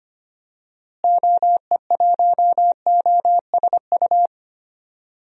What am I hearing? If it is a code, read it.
Morse code, heard as a keyed tone, about 25 wpm, 700 Hz. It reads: OE1OSU
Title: DXCC-100 Call Signs - 6 Characters - Repeated 2x - 25wpm